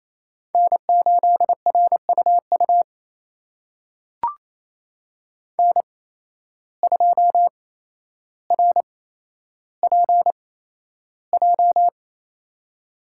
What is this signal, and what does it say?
Morse code, heard as a keyed tone, about 28 wpm, spaced as for 12 wpm, 700 Hz. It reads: N8RUU N2RPJ
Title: USA Call Signs - 5 Characters - 8x Character Spacing - 28wpm